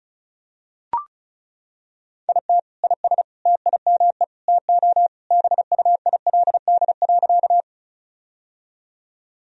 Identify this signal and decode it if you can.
Morse code, heard as a keyed tone, about 34 wpm, 700 Hz. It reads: IT IS TIME TO BUILD.
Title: Sentences from Top 1000 Words - Review Part 1 - 35wpm